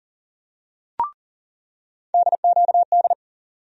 Morse code, heard as a keyed tone, about 40 wpm, 700 Hz. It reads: DQD
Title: USA Call Sign Postfixes - 3 Letters 40wpm